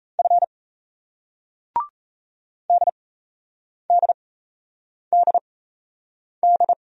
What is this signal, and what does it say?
Morse code, heard as a keyed tone, about 36 wpm, spaced as for 8 wpm, 700 Hz. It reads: F DDDD